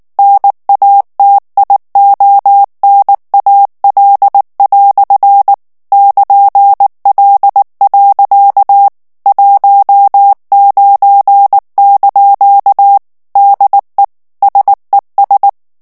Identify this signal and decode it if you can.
Morse code, heard as a keyed tone, about 19 wpm, 800 Hz. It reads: NATIONAL" <KN>L. 19) DE SES
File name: morse mp3 / FRANCAIS TEXTE CLAIR 1 VITESSE 18 MOTS MINUTES